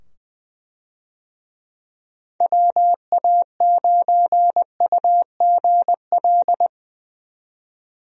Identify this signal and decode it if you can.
Morse code, heard as a keyed tone, about 20 wpm, 700 Hz. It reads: WA9UGL
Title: USA Call Signs - 6 Characters - 20wpm